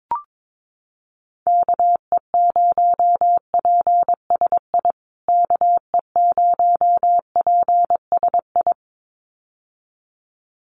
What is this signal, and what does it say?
Morse code, heard as a keyed tone, about 22 wpm, 700 Hz. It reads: KE0PSI KE0PSI
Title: USA Call Signs Repeated 2x 22wpm